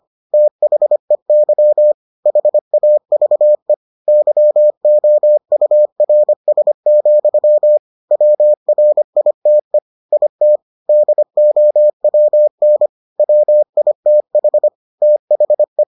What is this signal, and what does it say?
Morse code, heard as a keyed tone, about 25 wpm, 600 Hz. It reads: THEY HAVE YOURS, WRITE IT DOWN WITH THE